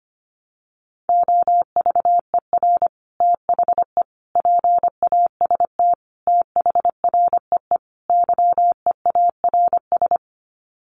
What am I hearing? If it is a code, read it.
Morse code, heard as a keyed tone, about 25 wpm, 700 Hz. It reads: OVER THE PAST THREE YEARS
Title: Sets of 5 Words Encore2 25wpm